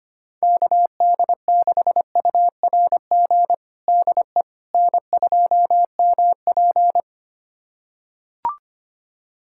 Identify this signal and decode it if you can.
Morse code, heard as a keyed tone, about 25 wpm, 700 Hz. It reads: KD6URG DE N2MP
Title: USA Callsign DE Callsign 25wpm